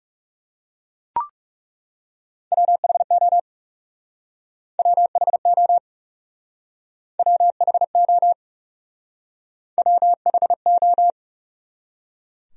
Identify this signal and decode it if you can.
Morse code, heard as a keyed tone, about 38 wpm, 700 Hz. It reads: WHO WHO WHO WHO